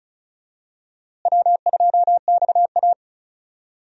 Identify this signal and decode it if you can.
Morse code, heard as a keyed tone, about 36 wpm, 700 Hz. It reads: W2XA
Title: USA Call Signs - 4 Characters - Repeated 2x - 35wpm